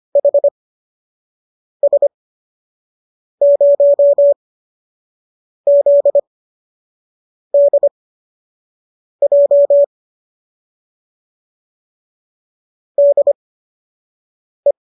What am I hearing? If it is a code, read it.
Morse code, heard as a keyed tone, about 25 wpm, spaced as for 6 wpm, 570 Hz. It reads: HS0ZDJ DE